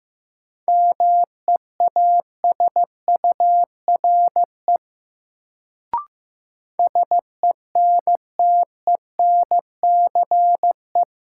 Morse code, heard as a keyed tone, about 15 wpm, 700 Hz. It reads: MEASURE SENTENCE
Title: Sentences from Top 1000 Words - Review Part 1 - New Word Drill - 15wpm